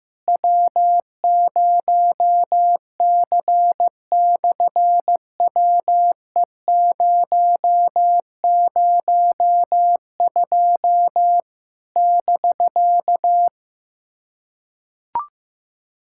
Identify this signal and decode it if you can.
Morse code, heard as a keyed tone, about 15 wpm, 700 Hz. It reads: W0C/WE002 <BK>
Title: SOTA QSO Practice 15wpm